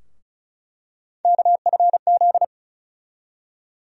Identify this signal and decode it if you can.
Morse code, heard as a keyed tone, about 36 wpm, 700 Hz. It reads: KFZ
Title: USA Call Sign Postfixes - 3 Letters 35wpm